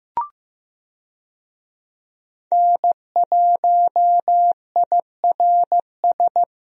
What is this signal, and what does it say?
Morse code, heard as a keyed tone, about 15 wpm, 700 Hz. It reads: N1IRS
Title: USA Call Signs - 5 Characters - Repeated 3x - 15wpm